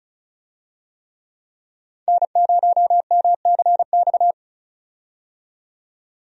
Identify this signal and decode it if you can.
Morse code, heard as a keyed tone, about 36 wpm, 700 Hz. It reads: N0MCX
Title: USA Call Signs - 5 Characters - 35wpm